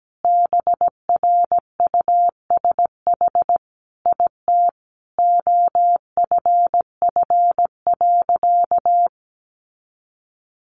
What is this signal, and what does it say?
Morse code, heard as a keyed tone, about 17 wpm, 700 Hz. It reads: BRUSH IT OFF.
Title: Sentences from Top 1100 Words 17wpm